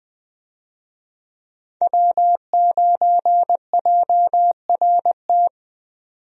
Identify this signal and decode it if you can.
Morse code, heard as a keyed tone, about 20 wpm, 700 Hz. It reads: W9JRT
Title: USA Call Signs - 5 Characters - 20wpm